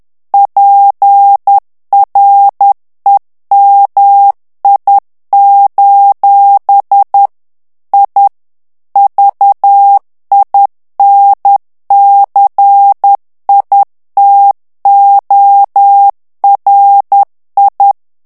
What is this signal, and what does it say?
Morse code, heard as a keyed tone, about 10 wpm, 800 Hz. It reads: PREMI: I VINCITORI
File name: ITALIEN texte clair 2 vitesse 10 mots minute